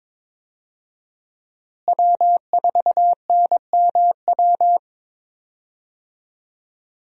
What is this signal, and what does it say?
Morse code, heard as a keyed tone, about 22 wpm, 700 Hz. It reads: W4NMW
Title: USA Call Signs - 5 Characters - Repeated 3x - 22wpm